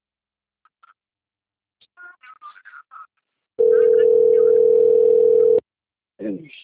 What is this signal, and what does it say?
Ringback tone (North America), 440 + 480 Hz, with speech.
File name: out- - - - - .66456